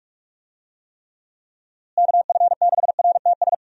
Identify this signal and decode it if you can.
Morse code, heard as a keyed tone, about 46 wpm, 700 Hz. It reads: KF6RTS
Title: USA Call Signs - 6 Characters - Repeated 2x - 45wpm